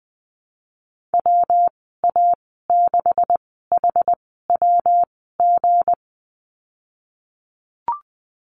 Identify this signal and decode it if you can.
Morse code, heard as a keyed tone, about 20 wpm, spaced as for 15 wpm, 700 Hz. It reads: WA6HWG
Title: 179 Lesson 29g - Slash - Call Signs 20wpm